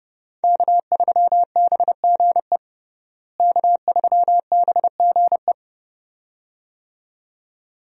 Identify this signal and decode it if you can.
Morse code, heard as a keyed tone, about 30 wpm, 700 Hz. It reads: K3BGE K3BGE
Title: USA Call Signs - 5 Characters - Repeated 2x - 30wpm